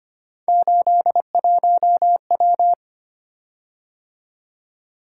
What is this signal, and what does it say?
Morse code, heard as a keyed tone, about 26 wpm, 700 Hz. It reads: 81W
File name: Three Letter-Number Combinations - 25wpm